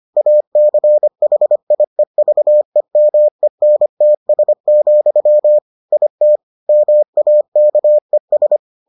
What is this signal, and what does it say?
Morse code, heard as a keyed tone, about 25 wpm, 600 Hz. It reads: ACHIEVEMENTS, IT MAKES